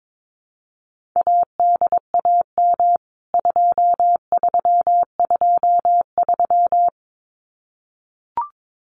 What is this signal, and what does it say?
Morse code, heard as a keyed tone, about 22 wpm, 700 Hz. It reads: ADAM 2323
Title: CWT Exchange 22wpm